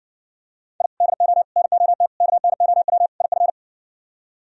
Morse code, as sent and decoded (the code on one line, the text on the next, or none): .. / -.. --- / -. --- - / -.- -. --- .-- / .. ..-.
I DO NOT KNOW IF